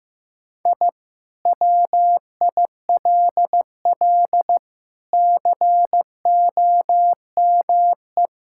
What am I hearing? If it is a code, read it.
Morse code, heard as a keyed tone, about 14 wpm, 700 Hz. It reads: I WILL COME
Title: Sentences from Top 1000 Words - Review Part 5 - 15wpm